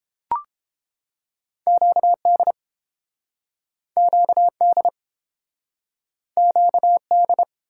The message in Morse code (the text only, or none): QD QD QD